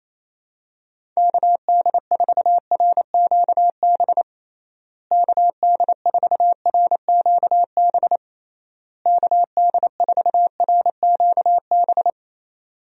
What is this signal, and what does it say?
Morse code, heard as a keyed tone, about 27 wpm, 700 Hz. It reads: KD4RQB KD4RQB KD4RQB
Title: USA Call Signs - 6 Characters - Repeated 3x - 28wpm